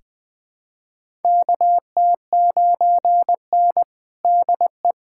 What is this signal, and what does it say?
Morse code, heard as a keyed tone, about 20 wpm, 700 Hz. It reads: KT9N DE
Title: USA Callsign DE Callsign - 4 Characters - 20wpm